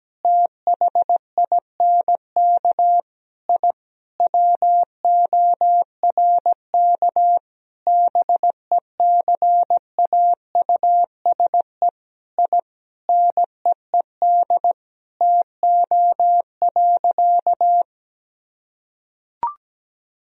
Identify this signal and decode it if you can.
Morse code, heard as a keyed tone, about 17 wpm, 700 Hz. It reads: THINK I WORK BECAUSE I NEED TO.
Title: Sentences from Top 1000 Words - Review Part 3 - 17wpm